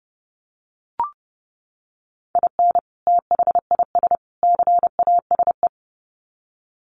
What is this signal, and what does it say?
Morse code, heard as a keyed tone, about 30 wpm, 700 Hz. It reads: IN THIS CASE
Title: Sets of 3 Words 30wpm